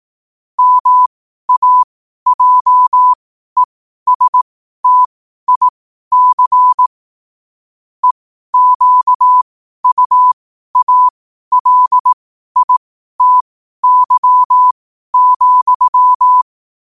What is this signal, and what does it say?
Morse code, heard as a keyed tone, about 18 wpm, spaced as for 12 wpm, 1000 Hz. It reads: MAJESTIC EQUALITY,
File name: qotdfiles / 12wpm